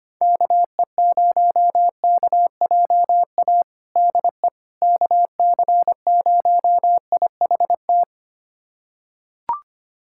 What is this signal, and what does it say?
Morse code, heard as a keyed tone, about 25 wpm, 700 Hz. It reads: KE0KJA DE KC0IHT
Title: USA Callsign DE Callsign - 6 Characters - 25wpm